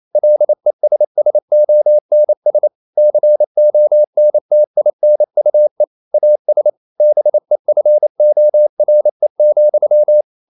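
Morse code, heard as a keyed tone, about 28 wpm, 600 Hz. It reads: LESSONS CONTINUE AS BEFORE,
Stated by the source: Morse code practice at 28 wpm, text of Ken Follett's novel Never in English, from CW med Gnister.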